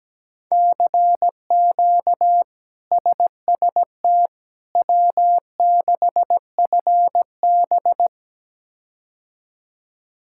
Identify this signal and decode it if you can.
Morse code, heard as a keyed tone, about 17 wpm, 700 Hz. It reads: CQ SST W6FB
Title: K1USN SST QSO Practice - 17wpm